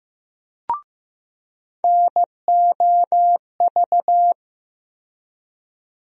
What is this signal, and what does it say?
Morse code, heard as a keyed tone, about 15 wpm, 700 Hz. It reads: NOV
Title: USA Call Sign Postfixes - 3 Letters 15wpm